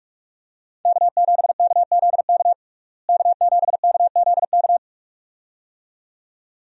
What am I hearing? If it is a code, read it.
Morse code, heard as a keyed tone, about 41 wpm, 700 Hz. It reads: K7KZK K7KZK
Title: USA Call Signs - 5 Characters - Repeated 2x - 45wpm